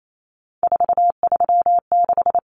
Morse code, heard as a keyed tone, about 28 wpm, 700 Hz. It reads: 436
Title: Common Cut Numbers - 3-Digit 28wpm